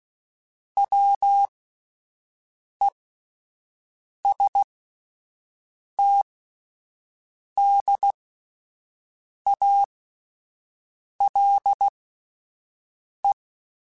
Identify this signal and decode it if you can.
Morse code, heard as a keyed tone, about 16 wpm, spaced as for 5 wpm, 800 Hz. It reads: WESTDALE